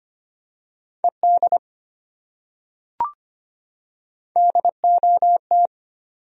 Morse code, heard as a keyed tone, about 25 wpm, 700 Hz. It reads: ED DOT
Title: USA Common Names - Max 3 Letters - 25wpm